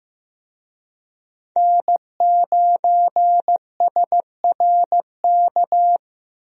Morse code, heard as a keyed tone, about 15 wpm, 700 Hz. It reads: N9SRK
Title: USA Call Signs - 5 Characters - Repeated 2x - 15wpm